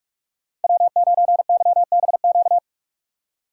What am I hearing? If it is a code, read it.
Morse code, heard as a keyed tone, about 43 wpm, 700 Hz. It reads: W9YBQ
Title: USA Call Signs - 2x Character Spacing - 45wpm